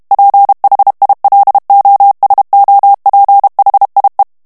Morse code, heard as a keyed tone, about 32 wpm, 800 Hz. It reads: PHILOSOPHIE
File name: FRANCAIS TEXTE CLAIR 2 VITESSE 30 MOTS MINUTES